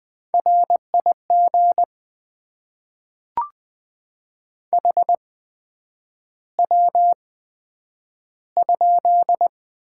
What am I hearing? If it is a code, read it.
Morse code, heard as a keyed tone, about 20 wpm, spaced as for 9 wpm, 700 Hz. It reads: RIG HW?
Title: Top 20 QSO Elements - 8x Character Spacing - 20wpm